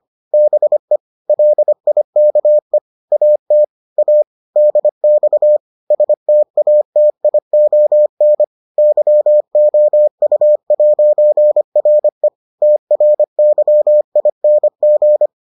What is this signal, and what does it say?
Morse code, heard as a keyed tone, about 25 wpm, 600 Hz. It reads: BE LIKE AT A DX STATION YOU'RE TRYING